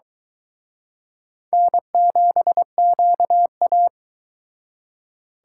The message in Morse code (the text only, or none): N7QA